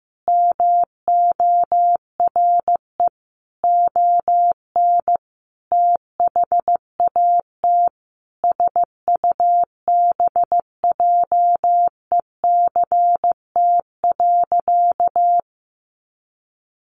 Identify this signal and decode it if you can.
Morse code, heard as a keyed tone, about 15 wpm, 700 Hz. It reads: MORE ON THAT SUBJECT.
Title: Sentences from Top 800 Words 15wpm